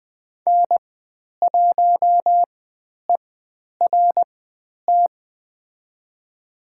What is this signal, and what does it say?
Morse code, heard as a keyed tone, about 20 wpm, spaced as for 10 wpm, 700 Hz. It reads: N1ERT
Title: 102 Lesson 17g - Letter u - Call Signs 20wpm